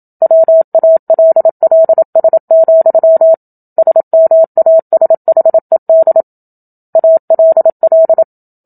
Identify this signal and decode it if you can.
Morse code, heard as a keyed tone, about 27 wpm, 650 Hz. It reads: WALLS, SMASHED ALL
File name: Book Two / 01-Chapter 1 - 27 WPM